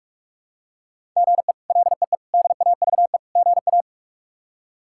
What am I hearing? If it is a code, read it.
Morse code, heard as a keyed tone, about 41 wpm, 700 Hz. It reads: GE LEE DAVE GA
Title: K1USN SST QSO Practice - 45wpm